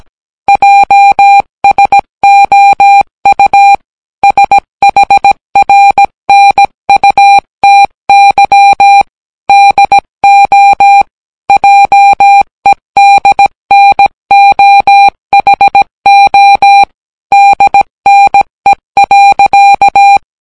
Morse code, heard as a keyed tone, about 17 wpm, 800 Hz. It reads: JSOU SHRNUTY DO JEDNOHO DNE.